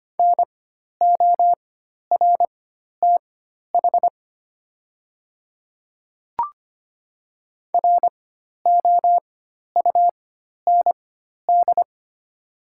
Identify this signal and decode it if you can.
Morse code, heard as a keyed tone, about 25 wpm, spaced as for 12 wpm, 690 Hz. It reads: NORTH ROUND
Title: 4x Character Spacing - Top 500 Words - 25wpm